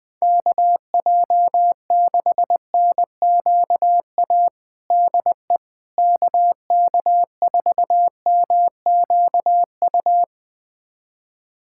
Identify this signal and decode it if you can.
Morse code, heard as a keyed tone, about 20 wpm, 700 Hz. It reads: KJ6NQA DE KK4MQU
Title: USA Callsign DE Callsign 20wpm